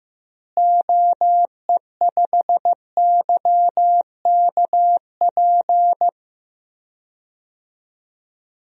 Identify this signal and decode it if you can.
Morse code, heard as a keyed tone, about 16 wpm, 700 Hz. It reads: OE5YKP
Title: DXCC-100 Call Signs - 6 Characters - Repeated 2x - 15wpm